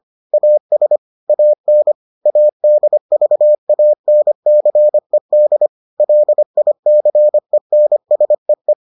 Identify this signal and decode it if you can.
Morse code, heard as a keyed tone, about 25 wpm, 600 Hz. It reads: AS AN ADVANCED LICENSEE